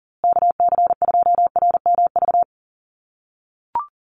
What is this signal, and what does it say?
Morse code, heard as a keyed tone, about 40 wpm, 700 Hz. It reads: KC2RMV
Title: USA Call Signs Repeated 3x 40wpm